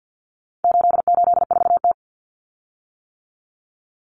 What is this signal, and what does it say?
Morse code, heard as a keyed tone, about 50 wpm, 700 Hz. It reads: 884T
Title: Common Cut Numbers - 4-Digit 50wpm